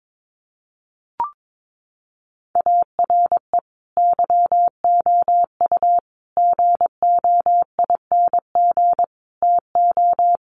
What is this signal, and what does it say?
Morse code, heard as a keyed tone, about 22 wpm, 700 Hz. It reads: ARE YOU GOING TO
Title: Sets of 5 Words 22wpm